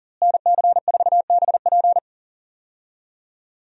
Morse code, heard as a keyed tone, about 40 wpm, 700 Hz. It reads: NC4BP
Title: USA Call Signs - 5 Characters - Repeated 2x - 40wpm